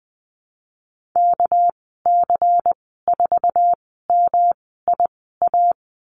Morse code, heard as a keyed tone, about 20 wpm, 700 Hz. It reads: K C 4 M I A
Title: 207 Lesson 33g - Letter j - Call Signs 20wpm